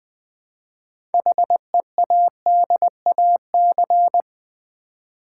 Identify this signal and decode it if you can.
Morse code, heard as a keyed tone, about 20 wpm, 700 Hz. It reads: HEADAC
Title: Build A Word - 3 Letters At A Time - 8-Letter Words - 20wpm